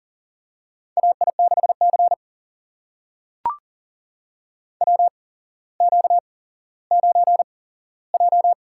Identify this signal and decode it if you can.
Morse code, heard as a keyed tone, about 38 wpm, spaced as for 17 wpm, 700 Hz. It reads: AI6C WQ9J
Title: USA Call Signs - 4 Characters - 8x Character Spacing - 40wpm